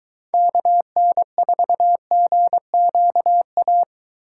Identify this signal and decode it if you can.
Morse code, heard as a keyed tone, about 21 wpm, 700 Hz. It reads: KN4GQA